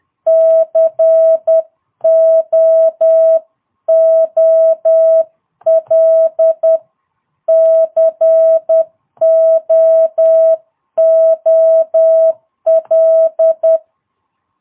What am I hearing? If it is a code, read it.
Morse code, heard as a keyed tone, about 10 wpm, 660 Hz. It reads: COOL COOL
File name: jh8cbh / 2022cw